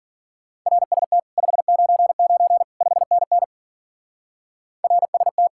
Morse code, heard as a keyed tone, about 49 wpm, 700 Hz. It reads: RST 599 5NN RST
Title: Speed-Racing Common QSO Phrases Encore - 40wpm